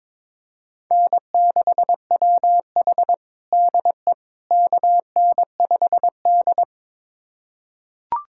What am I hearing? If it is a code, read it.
Morse code, heard as a keyed tone, about 22 wpm, 700 Hz. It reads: N6WH DE KN5D
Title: USA Callsign DE Callsign - 4 Characters - 22wpm